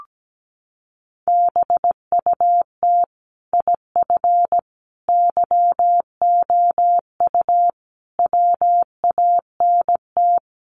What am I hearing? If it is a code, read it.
Morse code, heard as a keyed tone, about 17 wpm, 700 Hz. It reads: BUT IF YOU WANT
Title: Sets of 5 Words Encore2 17wpm